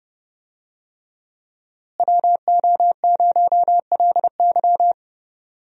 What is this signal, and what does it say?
Morse code, heard as a keyed tone, about 30 wpm, 700 Hz. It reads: WO0LY
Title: USA Call Signs - 5 Characters - Repeated 3x - 30wpm